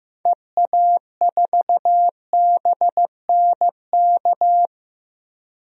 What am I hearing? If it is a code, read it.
Morse code, heard as a keyed tone, about 16 wpm, 700 Hz. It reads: EA4BNK
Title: DXCC-100 Call Signs - 6 Characters - Repeated 2x - 15wpm